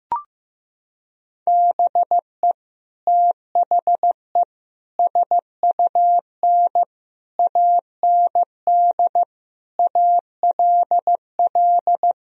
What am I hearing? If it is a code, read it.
Morse code, heard as a keyed tone, about 16 wpm, 700 Hz. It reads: BE THE SUN AND ALL
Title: Sentences from Top 1000 Words - Review Part 1 - 15wpm